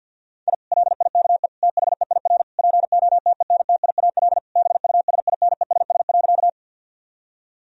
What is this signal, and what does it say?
Morse code, heard as a keyed tone, about 50 wpm, 700 Hz. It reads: I LIKE THEIR POTENTIAL BUSINESS.
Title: Sentences from Top 1000 Words - Review Part 4 - 50wpm